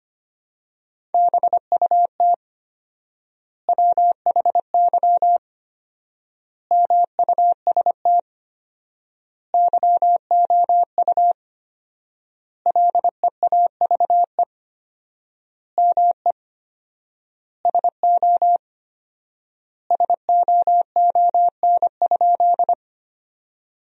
Morse code, heard as a keyed tone, about 25 wpm, 700 Hz. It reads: BUT WHY MUST YOU LEAVE ME SO SOON?
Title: 4x Word-Spacing - Sentences from Top 400 Words - 25wpm